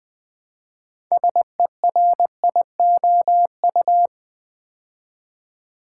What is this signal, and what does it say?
Morse code, heard as a keyed tone, about 20 wpm, 700 Hz. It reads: SERIOU
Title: Build A Word - 3 Letters At A Time - 9-Letter Words - 20wpm